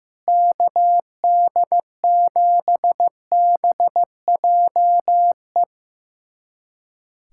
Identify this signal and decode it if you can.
Morse code, heard as a keyed tone, about 16 wpm, 700 Hz. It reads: KD7BJE
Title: USA Callsign DE Callsign 15wpm